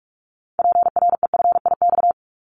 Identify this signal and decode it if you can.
Morse code, heard as a keyed tone, about 46 wpm, 700 Hz. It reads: PREFIX
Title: Callsign Prefixes for Top 25 DXCC - 45wpm